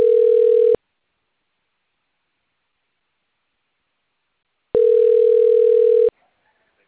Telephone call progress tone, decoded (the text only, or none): ringback tone